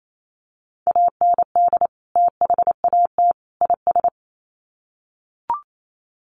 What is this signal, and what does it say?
Morse code, heard as a keyed tone, about 28 wpm, 700 Hz. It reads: AND THAT IS
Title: Sets of 3 Words 28wpm